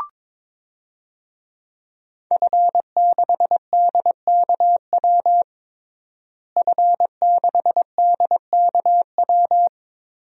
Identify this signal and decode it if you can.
Morse code, heard as a keyed tone, about 22 wpm, 700 Hz. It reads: F6DKW F6DKW
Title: DXCC-100 Call Signs - 5 Characters - Repeated 3x - 22wpm